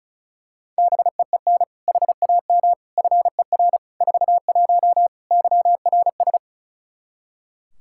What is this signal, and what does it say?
Morse code, heard as a keyed tone, about 34 wpm, 700 Hz. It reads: BEEN HAM FER 41 YRS
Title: K1USN SST QSO Practice 35wpm